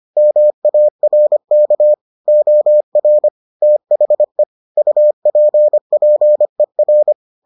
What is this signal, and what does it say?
Morse code, heard as a keyed tone, about 25 wpm, 600 Hz. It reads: MARK OR THE UPPER